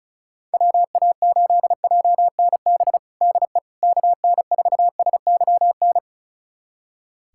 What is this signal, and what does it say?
Morse code, heard as a keyed tone, about 36 wpm, 700 Hz. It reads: WA8JNB DE KN4SYN
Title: USA Callsign DE Callsign - 6 Characters - 35wpm